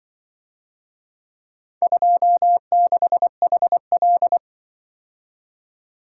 Morse code, heard as a keyed tone, about 23 wpm, 700 Hz. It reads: 26HL
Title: Speed-Racing - Four Letter-Number Combinations - 20wpm